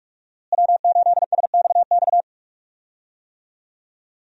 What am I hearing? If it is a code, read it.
Morse code, heard as a keyed tone, about 46 wpm, 700 Hz. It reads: W8SXX
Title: USA Call Signs - 5 Characters - Repeated 3x - 45wpm